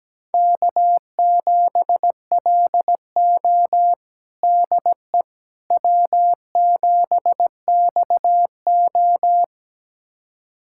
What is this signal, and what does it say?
Morse code, heard as a keyed tone, about 17 wpm, 700 Hz. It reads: K7LO DE W7XO
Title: USA Callsign DE Callsign - 4 Characters - 17wpm